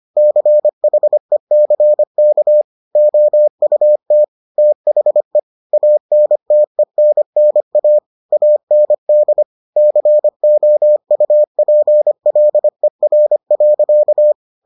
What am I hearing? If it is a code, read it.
Morse code, heard as a keyed tone, about 25 wpm, 600 Hz. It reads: CHECK OUT THE ANTENNA AND COUPLER.